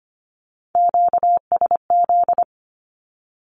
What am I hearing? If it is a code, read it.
Morse code, heard as a keyed tone, about 25 wpm, 700 Hz. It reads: QSZ
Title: Q-Codes 25wpm